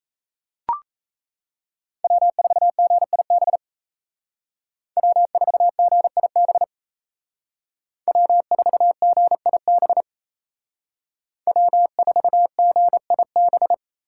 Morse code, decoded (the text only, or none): W4GIB W4GIB W4GIB W4GIB